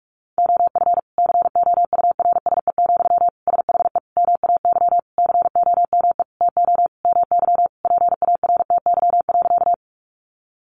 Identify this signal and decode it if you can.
Morse code, heard as a keyed tone, about 44 wpm, 700 Hz. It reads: OF COURSE, SHE MAY COME TO MY PARTY.
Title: Sentences from Top 1400 Words 45wpm